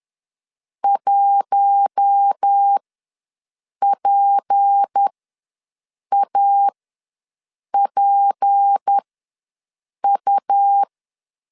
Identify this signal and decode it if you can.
Morse code, heard as a keyed tone, about 10 wpm, spaced as for 6 wpm, 800 Hz. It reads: 1PAPU
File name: LEZIONI / 6 PKX0 / 6 P ws 10wpm 3751